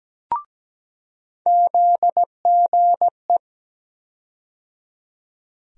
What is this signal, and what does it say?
Morse code, heard as a keyed tone, about 17 wpm, 700 Hz. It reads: ZGE
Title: USA Call Sign Postfixes - 3 Letters 17wpm